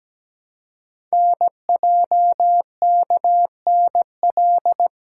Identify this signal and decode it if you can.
Morse code, heard as a keyed tone, about 17 wpm, 700 Hz. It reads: NJKNL
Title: Code Groups - 5 Letter:Numbers - 17wpm